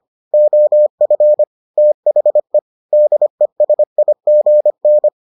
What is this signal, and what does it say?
Morse code, heard as a keyed tone, about 25 wpm, 600 Hz. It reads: OF THE DESIGN